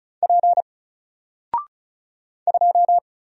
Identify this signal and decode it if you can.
Morse code, heard as a keyed tone, about 36 wpm, 700 Hz. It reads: P 2
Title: Single Letter-Number 35wpm